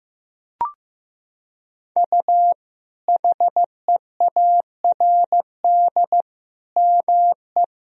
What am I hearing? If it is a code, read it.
Morse code, heard as a keyed tone, about 15 wpm, 700 Hz. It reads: U HEARD ME
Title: Sentences from Top 500 Words and Abbreviations 15wpm